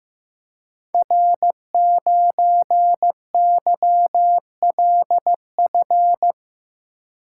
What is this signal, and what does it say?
Morse code, heard as a keyed tone, about 15 wpm, 700 Hz. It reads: R9YLF
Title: Code Groups - 5 Letter:Numbers - 15wpm